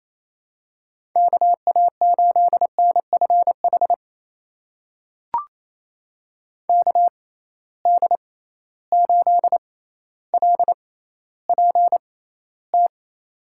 Morse code, heard as a keyed tone, about 27 wpm, spaced as for 14 wpm, 700 Hz. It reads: KA8NFH KD8LPT
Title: USA Call Signs - 6 Characters - 6x Character Spacing - 28wpm